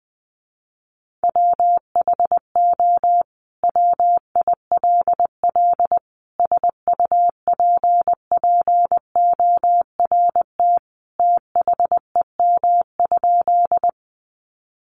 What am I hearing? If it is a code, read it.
Morse code, heard as a keyed tone, about 20 wpm, 700 Hz. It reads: WHO WILL SUPPORT THEM?
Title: Sentences from Top 800 Words 20wpm